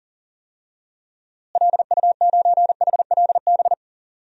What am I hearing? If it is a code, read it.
Morse code, heard as a keyed tone, about 40 wpm, 700 Hz. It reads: LU9HLB
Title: DXCC-100 Call Signs - 6 Characters - Repeated 2x - 40wpm